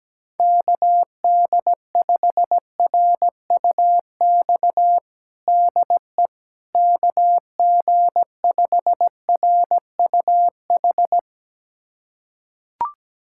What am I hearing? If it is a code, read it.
Morse code, heard as a keyed tone, about 17 wpm, 700 Hz. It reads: KD5RUX DE KG5RUH
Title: USA Callsign DE Callsign - 6 Characters - 17wpm